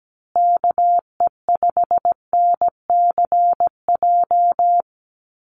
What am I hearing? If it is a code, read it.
Morse code, heard as a keyed tone, about 17 wpm, 700 Hz. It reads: KE5NCJ
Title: USA Call Signs Repeated 3x 17wpm